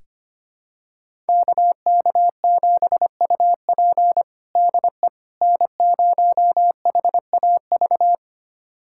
Morse code, heard as a keyed tone, about 25 wpm, 700 Hz. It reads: KK7UP DE N0HAV
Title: USA Callsign DE Callsign - 5 Characters - 25wpm